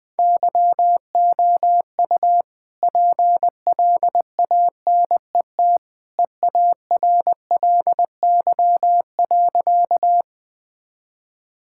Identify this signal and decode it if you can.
Morse code, heard as a keyed tone, about 20 wpm, 700 Hz. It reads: YOU PLANET EARLY.
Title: All-Age Appropriate Jokes - 20wpm